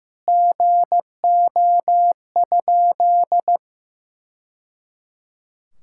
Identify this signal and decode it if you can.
Morse code, heard as a keyed tone, about 16 wpm, 700 Hz. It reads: GO?
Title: Sentences from Top 1000 Words - Review Part 5 - 15wpm